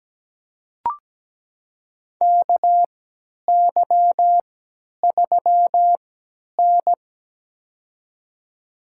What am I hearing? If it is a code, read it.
Morse code, heard as a keyed tone, about 17 wpm, spaced as for 10 wpm, 700 Hz. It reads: KY3N
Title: USA Call Signs - 4 Characters - 3x Character Spacing - 17wpm